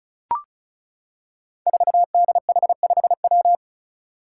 Morse code, heard as a keyed tone, about 36 wpm, 700 Hz. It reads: 4DH5W
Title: Five Challenging Letter-Number Combinations 35wpm